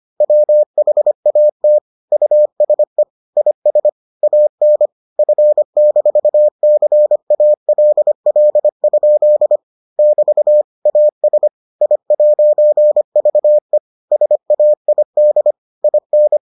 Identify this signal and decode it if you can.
Morse code, heard as a keyed tone, about 25 wpm, 600 Hz. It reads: WHAT USE IS AN F-CALL? <BT> AS I'VE SAID IN